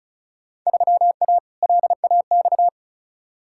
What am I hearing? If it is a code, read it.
Morse code, heard as a keyed tone, about 36 wpm, 700 Hz. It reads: 3A LAX
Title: Field Day Exchanges - 35wpm